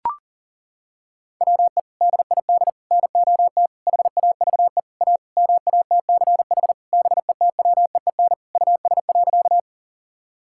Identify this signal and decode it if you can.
Morse code, heard as a keyed tone, about 40 wpm, 700 Hz. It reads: WE DID NOT HAVE A MATCH BETWEEN US.
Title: Sentences from Top 1600 Words 40wpm